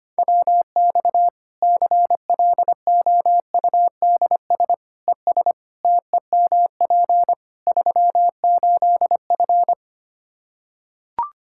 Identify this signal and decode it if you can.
Morse code, heard as a keyed tone, about 25 wpm, 700 Hz. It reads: WX CLOUDS ES TEMP 38F
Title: Practice QSO - Exchange 2 - Info - 25wpm